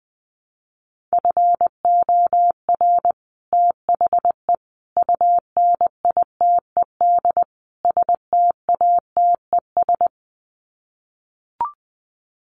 Morse code, heard as a keyed tone, about 20 wpm, 700 Hz. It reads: FOR THE UNITED STATES
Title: Sets of 4 Words 20wpm